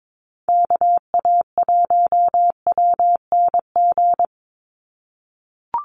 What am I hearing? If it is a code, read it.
Morse code, heard as a keyed tone, about 22 wpm, 700 Hz. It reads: KA1WNG
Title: USA Call Signs 22wpm